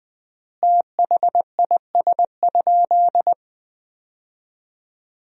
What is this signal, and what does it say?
Morse code, heard as a keyed tone, about 20 wpm, 700 Hz. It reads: THIS?
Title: 3x Word-Spacing - Sentences from Top 100 Words - 20wpm